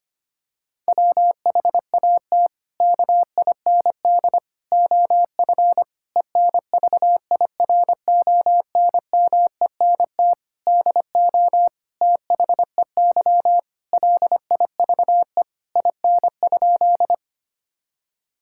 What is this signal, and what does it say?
Morse code, heard as a keyed tone, about 25 wpm, 700 Hz. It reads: WHAT KIND OF ENVIRONMENT DO THEY LIVE IN?
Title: Sentences from Top 1000 Words - Review Part 5 - 25wpm